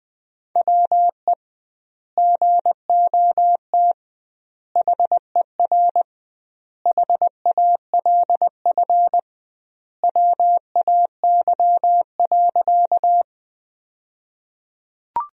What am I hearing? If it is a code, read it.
Morse code, heard as a keyed tone, about 21 wpm, 700 Hz. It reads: WE GOT HER HALF WAY.
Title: 2x Word-Spacing - Sentences from Top 400 Words - 20wpm